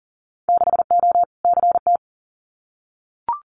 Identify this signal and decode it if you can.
Morse code, heard as a keyed tone, about 40 wpm, 700 Hz. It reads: 6O CT
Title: Winter Field Day Exchanges Encore 40wpm